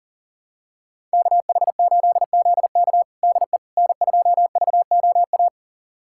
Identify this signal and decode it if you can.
Morse code, heard as a keyed tone, about 40 wpm, 700 Hz. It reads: KH8ZK DE N2VOA
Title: USA Callsign DE Callsign - 5 Characters - 40wpm